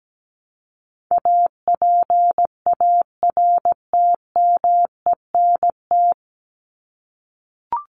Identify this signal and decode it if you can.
Morse code, heard as a keyed tone, about 17 wpm, 700 Hz. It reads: APARTMENT
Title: Top 751-1000 Words 17wpm